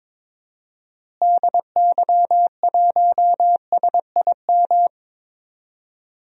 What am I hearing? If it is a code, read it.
Morse code, heard as a keyed tone, about 22 wpm, 700 Hz. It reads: DY1SIM
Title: DXCC-100 Call Signs - 6 Characters - Repeated 3x - 22wpm